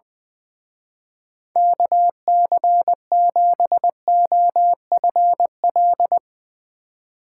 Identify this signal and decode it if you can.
Morse code, heard as a keyed tone, about 22 wpm, 700 Hz. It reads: KC7OFL
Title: Speed Racing - USA 2x3 Call Signs 20wpm